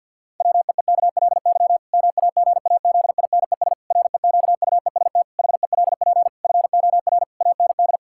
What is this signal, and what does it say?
Morse code, heard as a keyed tone, about 51 wpm, 700 Hz. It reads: WEEKLY MAGAZINES REQUEST HELP FOR AND
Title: Pangram Challenge v2 - 50wpm